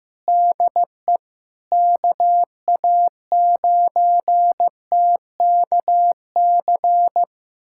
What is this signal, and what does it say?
Morse code, heard as a keyed tone, about 16 wpm, 700 Hz. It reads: DE KA9TKC
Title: USA Callsign DE Callsign 15wpm